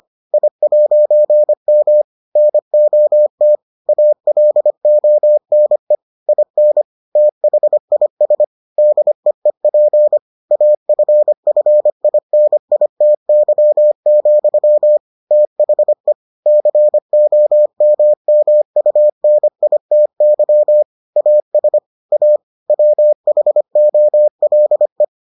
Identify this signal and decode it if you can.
Morse code, heard as a keyed tone, about 25 wpm, 600 Hz. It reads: I'M NOT ALONE IN THIS DEEP AFFINITY, THE COMMUNITY AS A WHOLE